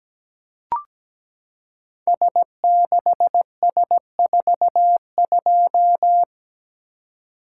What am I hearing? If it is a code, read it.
Morse code, heard as a keyed tone, about 17 wpm, 700 Hz. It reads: S6S42
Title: Code Groups - 5 Letter:Numbers - 17wpm